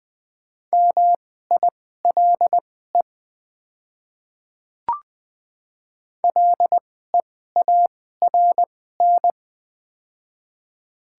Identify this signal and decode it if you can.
Morse code, heard as a keyed tone, about 20 wpm, spaced as for 14 wpm, 700 Hz. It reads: MILE LEARN
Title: 2x Character Spacing - Top 300 Words - 20wpm